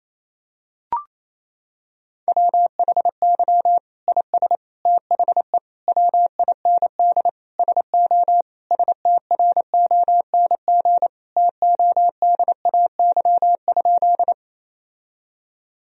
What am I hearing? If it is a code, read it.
Morse code, heard as a keyed tone, about 28 wpm, 700 Hz. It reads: WHY IS THE WIND SO STRONG TODAY?
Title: Sentences from Top 1000 Words - Review Part 2 - 28wpm